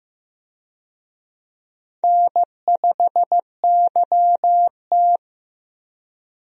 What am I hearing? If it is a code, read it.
Morse code, heard as a keyed tone, about 15 wpm, 700 Hz. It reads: N5YT
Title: USA Call Signs - 4 Characters - 15wpm